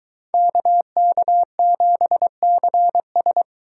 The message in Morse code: KK7CS

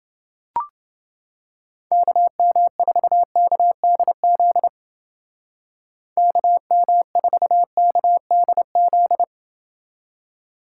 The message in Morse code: KM4KDZ KM4KDZ

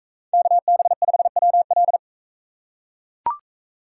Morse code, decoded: KB5WL

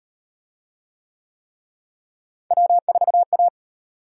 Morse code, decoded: W4A

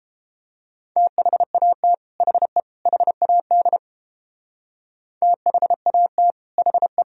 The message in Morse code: THAT HE HAD THAT HE

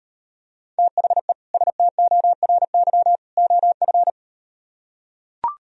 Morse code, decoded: THE STORY OF

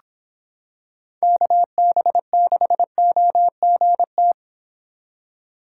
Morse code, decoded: KB6OGT

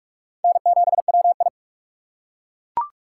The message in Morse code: N7WI

